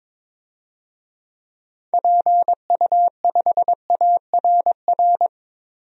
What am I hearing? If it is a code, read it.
Morse code, heard as a keyed tone, about 22 wpm, 700 Hz. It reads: PU5ARR
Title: DXCC-100 Call Signs - 6 Characters - Repeated 2x - 22wpm